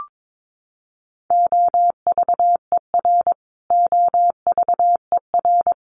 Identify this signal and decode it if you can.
Morse code, heard as a keyed tone, about 22 wpm, 700 Hz. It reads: OVER OVER
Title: Top 500 Words Repeated 22wpm